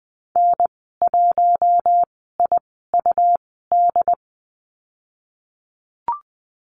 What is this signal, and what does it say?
Morse code, heard as a keyed tone, about 20 wpm, spaced as for 14 wpm, 700 Hz. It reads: N1IUD
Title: 140 Lesson 23g - Letter f - Call Signs 20wpm